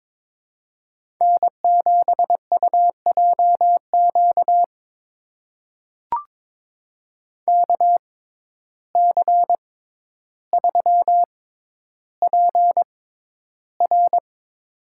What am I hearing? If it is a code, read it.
Morse code, heard as a keyed tone, about 22 wpm, spaced as for 11 wpm, 700 Hz. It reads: N7UJQ KC3PR
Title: USA Call Signs - 5 Characters - 6x Character Spacing - 22wpm